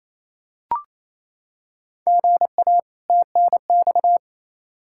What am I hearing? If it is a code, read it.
Morse code, heard as a keyed tone, about 28 wpm, 700 Hz. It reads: GA TNX
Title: Practice QSO - Exchange 1 - Beginning - 28wpm